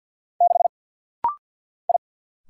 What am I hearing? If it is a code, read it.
Morse code, heard as a keyed tone, about 48 wpm, 700 Hz. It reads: 6 I
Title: Single Letter-Number - Rapid-Fire 50wpm